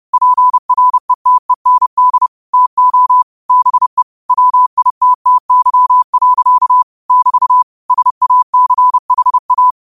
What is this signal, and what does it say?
Morse code, heard as a keyed tone, about 30 wpm, 1000 Hz. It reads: PRETEND TO BE WITTY. <BT> SACHA
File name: qotdfiles / 30wpm